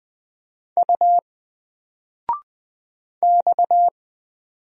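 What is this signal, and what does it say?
Morse code, heard as a keyed tone, about 20 wpm, 700 Hz. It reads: U X
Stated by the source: Single letter and number Morse drill at 20 wpm, from N4LCD.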